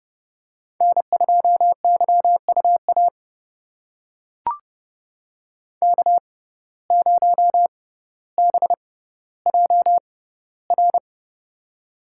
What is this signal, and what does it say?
Morse code, heard as a keyed tone, about 30 wpm, spaced as for 15 wpm, 700 Hz. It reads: N2YUA K0BJR
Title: USA Call Signs - 5 Characters - 6x Character Spacing - 30wpm